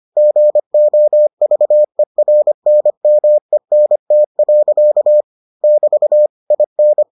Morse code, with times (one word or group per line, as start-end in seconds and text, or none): 0.2-5.2 GOVERNMENT.
5.6-6.3 <BT>
6.5-7.0 IN